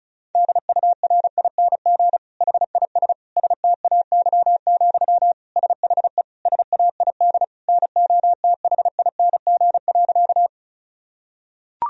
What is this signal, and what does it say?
Morse code, heard as a keyed tone, about 36 wpm, 700 Hz. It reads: DURING HIS STAY, SHE SAID NOTHING.
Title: Sentences from Top 1000 Words - Review Part 2 - 35wpm